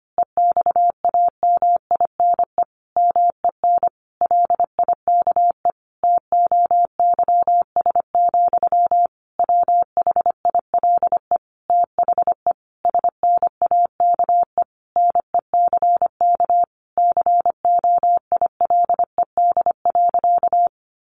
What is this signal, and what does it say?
Morse code, heard as a keyed tone, about 25 wpm, 700 Hz. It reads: EXAMINE MEN LIKE TOYS, WHILE THE SNAKE NECK COILED.
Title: A Sound Of Thunder 25wpm